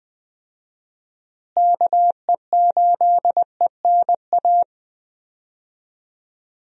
Morse code, heard as a keyed tone, about 21 wpm, 700 Hz. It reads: KE8ENA
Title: USA Call Signs - 6 Characters - Repeated 3x - 20wpm